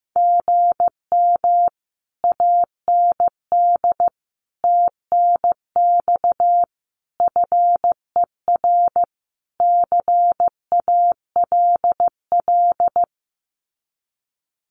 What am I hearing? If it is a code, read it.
Morse code, heard as a keyed tone, about 15 wpm, 700 Hz. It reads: GM AND TNX FER CALL
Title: Common QSO Phrases Encore 15wpm